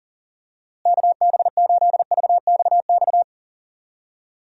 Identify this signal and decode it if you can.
Morse code, heard as a keyed tone, about 40 wpm, 700 Hz. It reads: KB8VXX
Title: USA Call Signs - 6 Characters - Repeated 2x - 40wpm